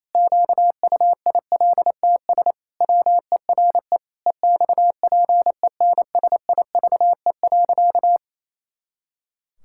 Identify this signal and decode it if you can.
Morse code, heard as a keyed tone, about 27 wpm, 700 Hz. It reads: QUILTS WERE EXPENSIVE.
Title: Pangram Challenge v2 - 28wpm